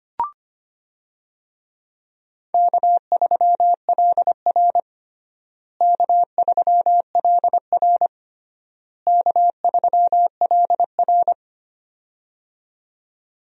Morse code, heard as a keyed tone, about 25 wpm, 700 Hz. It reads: K3LR K3LR K3LR
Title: USA Call Signs - 4 Characters - Repeated 3x - 25wpm